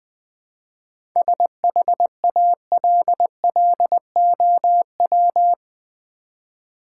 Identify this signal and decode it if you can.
Morse code, heard as a keyed tone, about 20 wpm, 700 Hz. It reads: SHALLOW
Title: 7 Letter Words - First 3 Letters Given - 20wpm